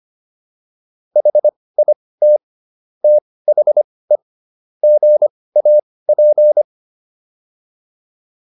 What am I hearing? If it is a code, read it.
Morse code, heard as a keyed tone, about 25 wpm, spaced as for 18 wpm, 600 Hz. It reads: HIT THE GAP